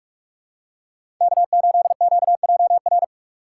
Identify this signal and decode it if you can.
Morse code, heard as a keyed tone, about 41 wpm, 690 Hz. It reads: K8QJR
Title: USA Call Signs - 5 Characters - 6x Character Spacing - 45wpm